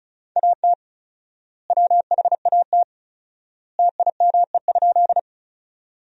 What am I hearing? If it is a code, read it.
Morse code, heard as a keyed tone, about 36 wpm, 700 Hz. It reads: AT WHAT TIME?
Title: 4x Word-Spacing - Sentences from Top 100 Words - 35wpm